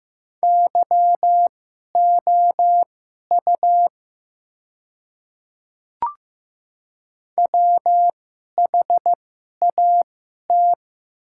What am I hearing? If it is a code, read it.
Morse code, heard as a keyed tone, about 15 wpm, spaced as for 11 wpm, 700 Hz. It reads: YOU WHAT
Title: 2x Character Spacing - Top 100 Words - 15wpm